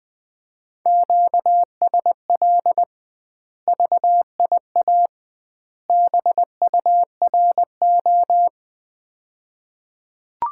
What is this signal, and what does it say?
Morse code, heard as a keyed tone, about 21 wpm, 700 Hz. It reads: QSL VIA BURO
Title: Common QSO Phrases - 2x Word-Spacing - 20wpm